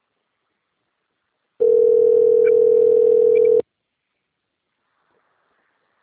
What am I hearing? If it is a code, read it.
Ringback tone (North America), 440 + 480 Hz.
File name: rg-1600-unknown- - - .50621